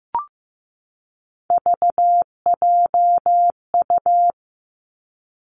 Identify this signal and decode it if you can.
Morse code, heard as a keyed tone, about 15 wpm, 700 Hz. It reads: VJU
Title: Three Challenging Letter-Number Combinations 15wpm